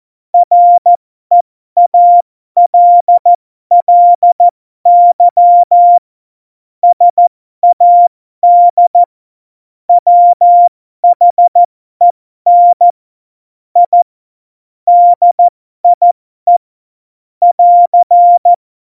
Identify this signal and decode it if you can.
Morse code, heard as a keyed tone, about 14 wpm, spaced as for 12 wpm, 700 Hz. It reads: REALLY SAD WHEN I DIE <AR>
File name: Morsekurs / Witze